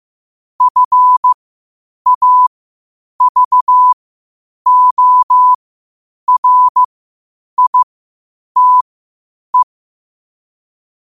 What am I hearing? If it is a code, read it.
Morse code, heard as a keyed tone, about 15 wpm, spaced as for 8 wpm, 1000 Hz. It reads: FAVORITE
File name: qotdfiles / 8wpm